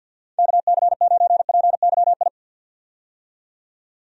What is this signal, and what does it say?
Morse code, heard as a keyed tone, about 47 wpm, 700 Hz. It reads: KC9PYI
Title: USA Call Signs - 6 Characters - 50wpm